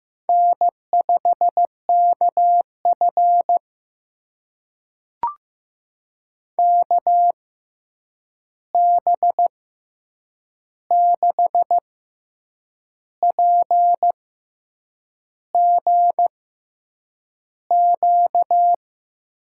Morse code, heard as a keyed tone, about 15 wpm, spaced as for 7 wpm, 700 Hz. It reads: N5KF KB6PGQ
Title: USA Call Signs - 6x Character Spacing - 15wpm